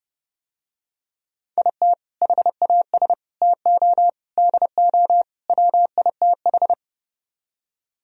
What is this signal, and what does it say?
Morse code, heard as a keyed tone, about 30 wpm, 700 Hz. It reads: IT HAS TO DO WITH